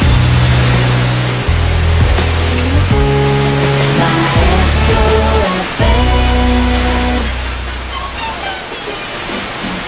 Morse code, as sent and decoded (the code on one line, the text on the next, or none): none